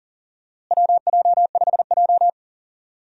Morse code, heard as a keyed tone, about 40 wpm, 700 Hz. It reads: WJ5J
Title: USA Call Signs - 4 Characters - Repeated 2x - 40wpm